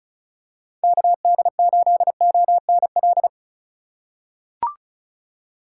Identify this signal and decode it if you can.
Morse code, heard as a keyed tone, about 36 wpm, 700 Hz. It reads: KD8ONL
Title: USA Call Signs - 6 Characters - 3x Character Spacing - 35wpm